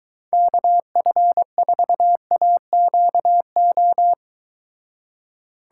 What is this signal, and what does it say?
Morse code, heard as a keyed tone, about 23 wpm, 700 Hz. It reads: KF4AQO